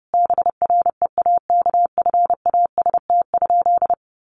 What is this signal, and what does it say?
Morse code, heard as a keyed tone, about 30 wpm, 700 Hz. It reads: BREAKFAST?
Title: Sentences from Top 1500 Words 30wpm